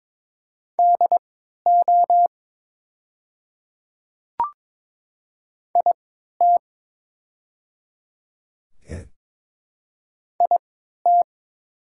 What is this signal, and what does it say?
Morse code, heard as a keyed tone, about 23 wpm, spaced as for 13 wpm, 700 Hz. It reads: DO IT IT
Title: 3x Character Spacing - Top 100 Words - 22wpm